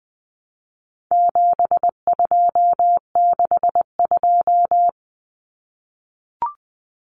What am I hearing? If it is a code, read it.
Morse code, heard as a keyed tone, about 20 wpm, 700 Hz. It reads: 7262
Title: Common Cut Numbers - 4-Digit 20wpm